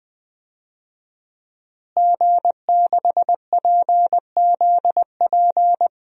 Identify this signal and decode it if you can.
Morse code, heard as a keyed tone, about 20 wpm, 700 Hz. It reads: G6PZP
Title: DXCC-100 Call Signs - 5 Characters - Repeated 2x - 20wpm